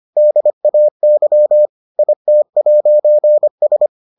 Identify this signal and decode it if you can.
Morse code, heard as a keyed tone, about 25 wpm, 600 Hz. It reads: DAY IT'S